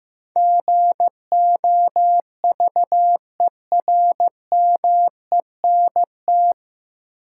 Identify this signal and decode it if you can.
Morse code, heard as a keyed tone, about 15 wpm, 700 Hz. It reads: GOVERMENT
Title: Sentences from Top 1300 Words - New Word Drill Alt - 15wpm